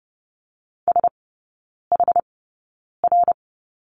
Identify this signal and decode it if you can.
Morse code, heard as a keyed tone, about 30 wpm, spaced as for 9 wpm, 700 Hz. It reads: SHR
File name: Lesson 19e - Letter w - Three Characters 30wpm